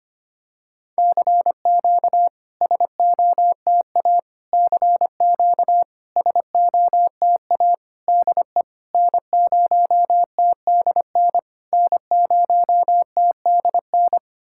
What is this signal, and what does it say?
Morse code, heard as a keyed tone, about 25 wpm, 690 Hz. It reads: CQ SOTA CQ SOTA DE N0TDN N0TDN
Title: SOTA QSO Practice 25wpm